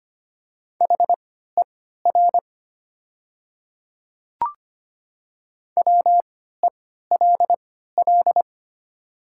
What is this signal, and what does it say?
Morse code, heard as a keyed tone, about 25 wpm, spaced as for 14 wpm, 700 Hz. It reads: HER WELL
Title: 3x Character Spacing - Top 200 Words - 25wpm